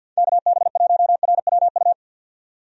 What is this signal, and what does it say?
Morse code, heard as a keyed tone, about 52 wpm, 700 Hz. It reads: KB1RWU
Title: USA Call Signs - 6 Characters - 50wpm